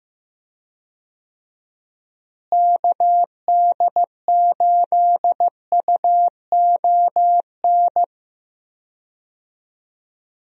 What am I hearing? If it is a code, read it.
Morse code, heard as a keyed tone, about 15 wpm, 700 Hz. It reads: KD8UON
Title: USA Call Signs - 6 Characters - Repeated 2x - 15wpm